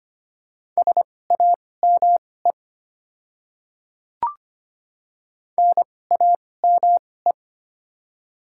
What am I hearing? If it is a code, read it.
Morse code, heard as a keyed tone, about 25 wpm, spaced as for 18 wpm, 700 Hz. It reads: SAME NAME
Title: 2x Character Spacing - Top 200 Words - 25wpm